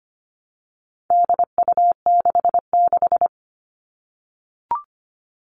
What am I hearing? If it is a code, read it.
Morse code, heard as a keyed tone, about 25 wpm, 700 Hz. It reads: DU66
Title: All Cut Numbers - 4-Digit 25wpm